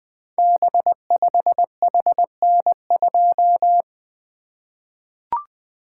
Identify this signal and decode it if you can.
Morse code, heard as a keyed tone, about 20 wpm, 700 Hz. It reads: B5HN2
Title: Code Groups - 5 Letter:Numbers - 20wpm